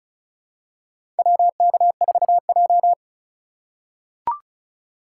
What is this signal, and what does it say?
Morse code, heard as a keyed tone, about 34 wpm, 700 Hz. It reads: WK4J
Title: USA Call Signs - 4 Characters - 2x Character Spacing - 35wpm